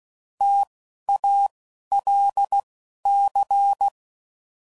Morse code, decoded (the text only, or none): TALC